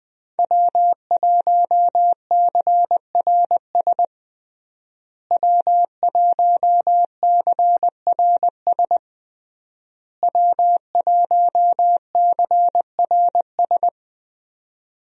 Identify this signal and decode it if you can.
Morse code, heard as a keyed tone, about 20 wpm, 700 Hz. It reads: W1CRS W1CRS W1CRS
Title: USA Call Signs - 5 Characters - Repeated 3x - 20wpm